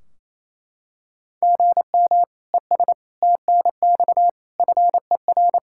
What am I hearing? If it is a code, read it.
Morse code, heard as a keyed tone, about 28 wpm, 700 Hz. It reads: GM ES TNX FER
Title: Practice QSO - Exchange 1 - Beginning - 28wpm